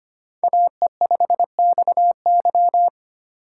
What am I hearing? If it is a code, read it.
Morse code, heard as a keyed tone, about 25 wpm, 700 Hz. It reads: AE5XY
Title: USA Call Signs - 5 Characters - 2x Character Spacing - 25wpm